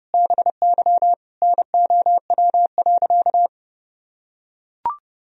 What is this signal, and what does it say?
Morse code, heard as a keyed tone, about 30 wpm, 700 Hz. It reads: BY NOW.
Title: Sentences from Top 1000 Words - Review Part 3 - 30wpm